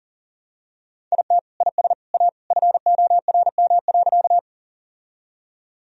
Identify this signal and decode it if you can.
Morse code, heard as a keyed tone, about 37 wpm, 700 Hz. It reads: IT IS A FORM.
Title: Sentences from Top 1000 Words - Review Part 1 - 40wpm